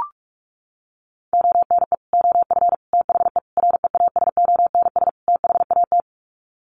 Morse code, heard as a keyed tone, about 44 wpm, 700 Hz. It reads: ONE OF THE REASONS THAT
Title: Sets of 5 Words Encore 45wpm